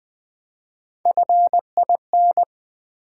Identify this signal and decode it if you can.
Morse code, heard as a keyed tone, about 20 wpm, 700 Hz. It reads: FIN
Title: Build A Word - 1 Letter At A Time - 4-Letter Words - 20wpm